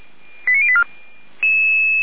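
Five-tone selective call (ZVEI), heard as quick runs of five tones, tones of about 70 ms. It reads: 89084
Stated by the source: ZVEI five-tone call received on an SDR, fire brigade Orlamünde (feuerwehr-orlamuende) recording archive.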